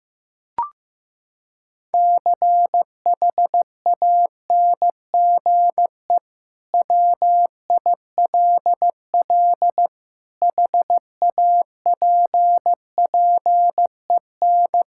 Morse code, heard as a keyed tone, about 15 wpm, 700 Hz. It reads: CHANGE WILL HAPPEN
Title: Sentences from Top 1000 Words - Review Part 5 - 15wpm